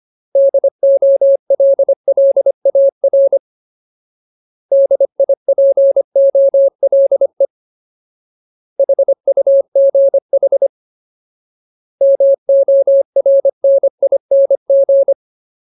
Morse code, heard as a keyed tone, about 25 wpm, 550 Hz. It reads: DOLLAR DIPOLE HUGH MORNING
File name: Ultimate Word List / 25WPM Ultimate